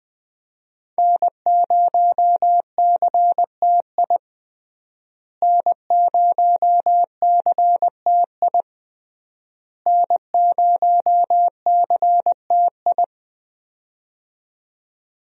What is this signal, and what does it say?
Morse code, heard as a keyed tone, about 20 wpm, 700 Hz. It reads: N0CTI N0CTI N0CTI
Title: USA Call Signs - 5 Characters - Repeated 3x - 20wpm